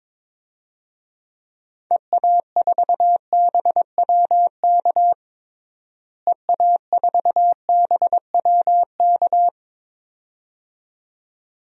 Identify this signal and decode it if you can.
Morse code, heard as a keyed tone, about 22 wpm, 700 Hz. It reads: EA4BWK EA4BWK
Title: DXCC-100 Call Signs - 6 Characters - Repeated 2x - 22wpm